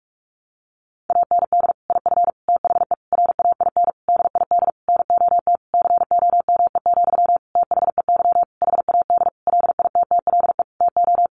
Morse code, heard as a keyed tone, about 44 wpm, 700 Hz. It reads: AND IF THE RAIN DID NOT COME, THEY HAD LITTLE TO
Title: Sentences from Top 1300 Words 45wpm